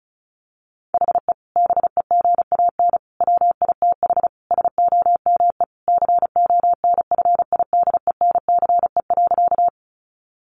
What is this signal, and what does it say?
Morse code, heard as a keyed tone, about 34 wpm, 700 Hz. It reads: HE BEGAN WITH SOME CONFIDENCE.
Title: Sentences from Top 1500 Words 35wpm